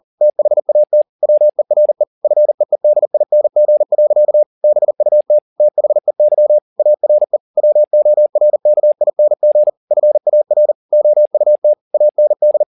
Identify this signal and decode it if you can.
Morse code, heard as a keyed tone, about 40 wpm, 600 Hz. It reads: THAT WERE FEEDING. BUT THEY ARE WORKING FAR OUT AND